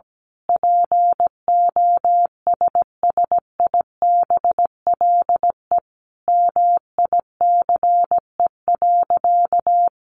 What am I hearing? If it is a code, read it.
Morse code, heard as a keyed tone, about 17 wpm, 700 Hz. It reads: POSSIBLE MICE.
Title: A Sound Of Thunder 17wpm